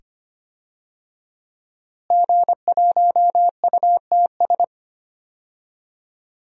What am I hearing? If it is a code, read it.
Morse code, heard as a keyed tone, about 25 wpm, 700 Hz. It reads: G1UTS
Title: DXCC-100 Call Signs - 5 Characters - Repeated 2x - 25wpm